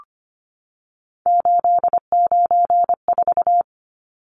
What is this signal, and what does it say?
Morse code, heard as a keyed tone, about 25 wpm, 690 Hz. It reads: 894
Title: Numbers - 3-Digit 25wpm